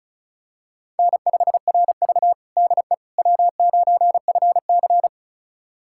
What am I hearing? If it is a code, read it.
Morse code, heard as a keyed tone, about 36 wpm, 700 Hz. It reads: N5RV DE W9FC
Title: USA Callsign DE Callsign - 4 Characters - 35wpm